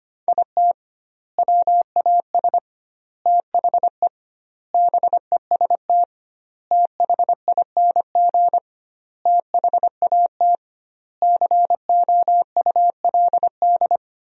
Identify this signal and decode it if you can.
Morse code, heard as a keyed tone, about 25 wpm, 700 Hz. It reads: IT WAS THE BEST THING THAT COULD
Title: 2x Word-Spacing - Sentences from Top 400 Words - 25wpm